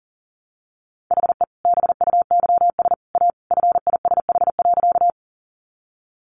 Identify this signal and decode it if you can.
Morse code, heard as a keyed tone, about 40 wpm, 700 Hz. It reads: HE BUYS A FISH.
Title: Easy Phrases 40wpm